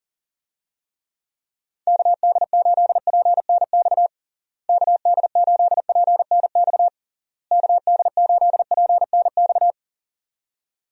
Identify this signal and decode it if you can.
Morse code, heard as a keyed tone, about 40 wpm, 700 Hz. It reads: KD8PNX KD8PNX KD8PNX
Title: USA Call Signs - 6 Characters - Repeated 3x - 40wpm